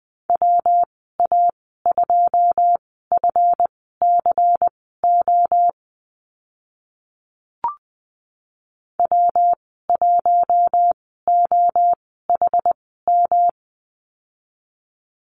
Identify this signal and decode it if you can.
Morse code, heard as a keyed tone, about 20 wpm, spaced as for 14 wpm, 700 Hz. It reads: WA2FCO W1OHM
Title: 167 Lesson 27g - Number 7 - Call Signs 20wpm